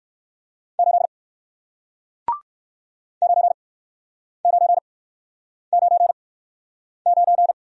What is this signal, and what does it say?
Morse code, heard as a keyed tone, about 60 wpm, spaced as for 9 wpm, 700 Hz. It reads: 8 9999